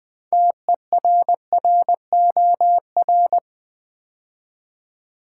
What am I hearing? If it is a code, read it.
Morse code, heard as a keyed tone, about 20 wpm, 700 Hz. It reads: TERROR
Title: Build A Word - 3 Letters At A Time - 9-Letter Words - 20wpm